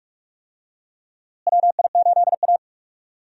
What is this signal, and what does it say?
Morse code, heard as a keyed tone, about 42 wpm, 700 Hz. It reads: WI8A
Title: USA Call Signs - 4 Characters - Repeated 3x - 45wpm